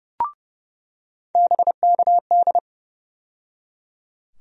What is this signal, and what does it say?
Morse code, heard as a keyed tone, about 30 wpm, 700 Hz. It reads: BKD
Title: USA Call Sign Postfixes - 3 Letters 30wpm